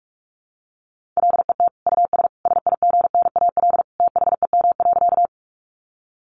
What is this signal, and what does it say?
Morse code, heard as a keyed tone, about 44 wpm, 700 Hz. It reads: LET US SIGNAL THEM.
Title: Sentences from Top 1400 Words 45wpm